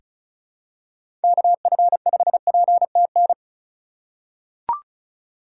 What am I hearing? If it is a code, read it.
Morse code, heard as a keyed tone, about 36 wpm, 700 Hz. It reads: KF5PTN
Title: USA Call Signs - 4x Character Spacing - 35wpm